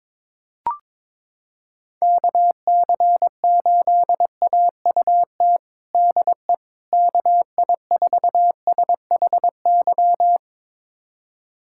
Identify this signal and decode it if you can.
Morse code, heard as a keyed tone, about 22 wpm, 700 Hz. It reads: KC8AUT DE KI4SHY
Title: USA Callsign DE Callsign - 6 Characters - 22wpm